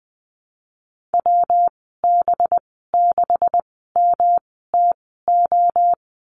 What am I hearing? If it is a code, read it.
Morse code, heard as a keyed tone, about 20 wpm, spaced as for 15 wpm, 700 Hz. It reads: WB6MTO
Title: 193 Lesson 31g - Letter v - Call Signs 20wpm